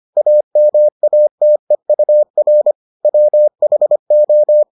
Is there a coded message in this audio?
Yes: Morse code from 0.2 to 4.6 s, heard as a keyed tone, about 25 wpm, 600 Hz. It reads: AMATEUR WHO